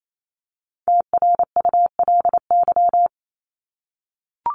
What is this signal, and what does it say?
Morse code, heard as a keyed tone, about 28 wpm, 700 Hz. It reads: TRULY
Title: Top 1501-1750 Words 28wpm